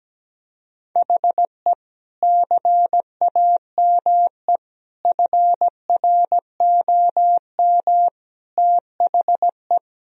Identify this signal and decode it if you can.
Morse code, heard as a keyed tone, about 17 wpm, 700 Hz. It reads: HE CAME FROM THE
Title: Sentences from Top 1000 Words - Review Part 1 - 17wpm